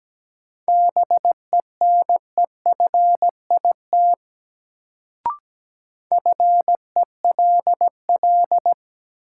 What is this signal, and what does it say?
Morse code, heard as a keyed tone, about 17 wpm, 700 Hz. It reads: BENEFIT FELL
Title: Sets of 5 Words Encore - New Word Drill - 17wpm